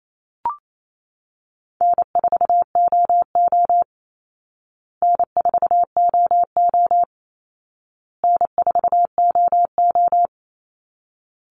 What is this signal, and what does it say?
Morse code, heard as a keyed tone, about 28 wpm, 700 Hz. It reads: N4OO N4OO N4OO
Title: Easy Call Signs Repeated 3x - 28wpm